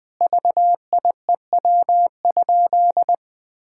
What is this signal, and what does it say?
Morse code, heard as a keyed tone, about 20 wpm, 700 Hz. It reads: VIEW?
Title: Sentences from Top 1000 Words - Review Part 6 - 20wpm